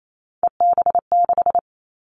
Morse code, heard as a keyed tone, about 28 wpm, 700 Hz. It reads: EB6
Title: Callsign Prefixes for Top 50 DXCC - 28wpm